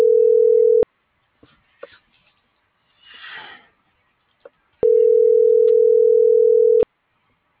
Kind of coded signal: telephone call progress tone

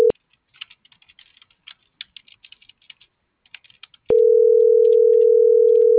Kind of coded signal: telephone call progress tone

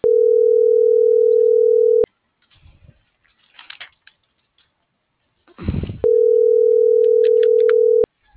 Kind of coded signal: telephone call progress tone

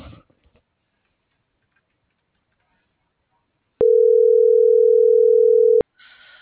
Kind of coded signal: telephone call progress tone

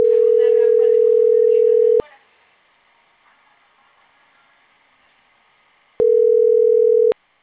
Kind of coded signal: telephone call progress tone

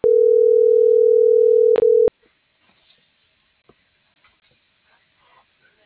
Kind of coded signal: telephone call progress tone

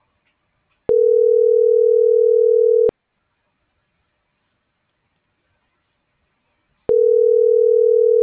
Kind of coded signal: telephone call progress tone

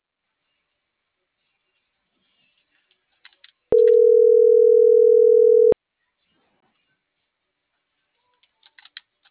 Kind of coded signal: telephone call progress tone

